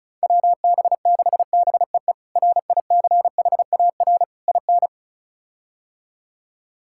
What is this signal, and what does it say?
Morse code, heard as a keyed tone, about 36 wpm, 700 Hz. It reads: WB6BEE RICHAR IN
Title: K1USN SST QSO Practice - 35wpm